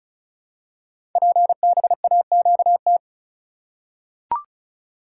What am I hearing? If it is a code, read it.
Morse code, heard as a keyed tone, about 36 wpm, 690 Hz. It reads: PBAQT
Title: Code Groups - 5 Letters - 35wpm